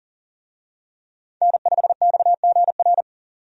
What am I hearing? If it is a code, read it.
Morse code, heard as a keyed tone, about 40 wpm, 700 Hz. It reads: N5XGR
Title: USA Call Signs - 5 Characters - 40wpm